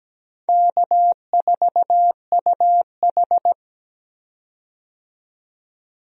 Morse code, heard as a keyed tone, about 17 wpm, 700 Hz. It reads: K4UH
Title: USA Call Signs - 4 Characters - 17wpm